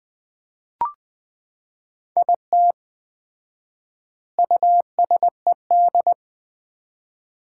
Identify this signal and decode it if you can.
Morse code, heard as a keyed tone, about 20 wpm, 700 Hz. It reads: IT USED
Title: 4x Word-Spacing - Sets of 4 Words - 20wpm